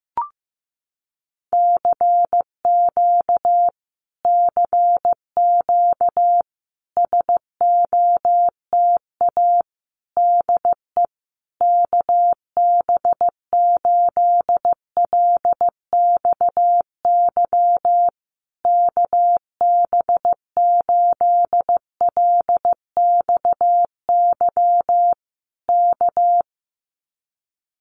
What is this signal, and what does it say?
Morse code, heard as a keyed tone, about 15 wpm, 700 Hz. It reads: CQ CQ SOTA DE KB8LXY KB8LXY K
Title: Common QSO Phrases Encore 15wpm